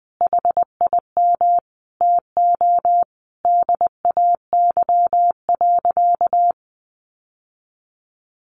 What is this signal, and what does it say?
Morse code, heard as a keyed tone, about 20 wpm, 700 Hz. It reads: HIM TO DAY.
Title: Sentences from Top 1100 Words 20wpm